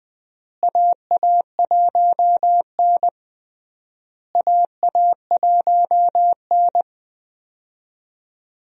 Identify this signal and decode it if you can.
Morse code, heard as a keyed tone, about 20 wpm, 700 Hz. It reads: AA1N AA1N
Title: USA Call Signs - 4 Characters - Repeated 3x - 20wpm